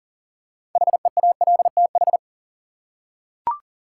Morse code, heard as a keyed tone, about 40 wpm, 700 Hz. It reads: HEALTH
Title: Six Letter Words - 40wpm